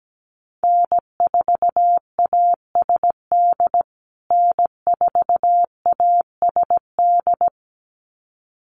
Morse code, heard as a keyed tone, about 17 wpm, 700 Hz. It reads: N4ASD N4ASD
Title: USA Call Signs Repeated 2x 17wpm